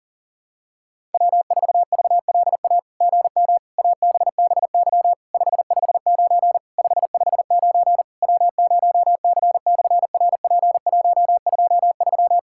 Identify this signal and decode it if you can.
Morse code, heard as a keyed tone, about 40 wpm, 700 Hz. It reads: W4VLA GM ABBY 559 559 W0C/RP123
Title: SOTA QSO Practice 40wpm